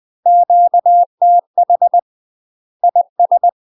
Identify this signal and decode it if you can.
Morse code, heard as a keyed tone, about 20 wpm, 700 Hz. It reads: QTH IS
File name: UK MASTER QSOs TRAINING / 20WPM / UK MASTER QSOs 0059